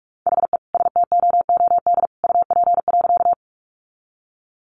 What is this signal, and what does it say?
Morse code, heard as a keyed tone, about 44 wpm, 700 Hz. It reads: HE STOOD UP.
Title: Sentences from Top 500 Words 45wpm